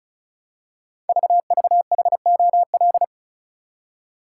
Morse code, heard as a keyed tone, about 36 wpm, 700 Hz. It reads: VVHOL
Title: Code Groups - 5 Letters - 35wpm